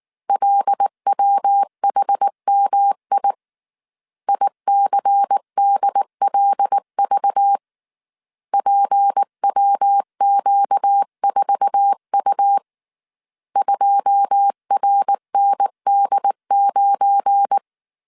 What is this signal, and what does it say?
Morse code, heard as a keyed tone, about 19 wpm, 800 Hz. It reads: LWHMI ICDLV PWQ4U 2RND9
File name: CW / DET / 1 9 1500C S A 18wpm 2459